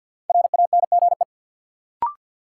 Morse code, heard as a keyed tone, about 52 wpm, 700 Hz. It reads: RANGE
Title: Sets of 4 Words - New Word Drill - 50wpm